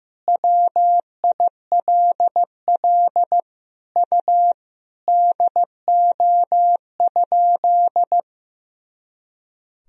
Morse code, heard as a keyed tone, about 15 wpm, 700 Hz. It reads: WILL U DO?
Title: Sentences from Top 100 Words and Abbreviations 15wpm